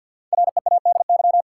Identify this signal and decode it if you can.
Morse code, heard as a keyed tone, about 50 wpm, 700 Hz. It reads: READY
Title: Sentences from Top 400 Words - New Word Drill - 50wpm